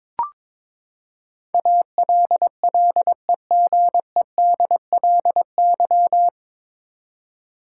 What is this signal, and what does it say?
Morse code, heard as a keyed tone, about 22 wpm, 700 Hz. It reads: ALLEGEDLY
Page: Morse Code Ninja practice 22wpm